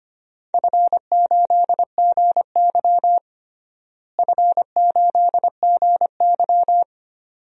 Morse code, heard as a keyed tone, about 25 wpm, 700 Hz. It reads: F8GY F8GY
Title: DXCC-100 Call Signs - 4 Characters - Repeated 2x - 25wpm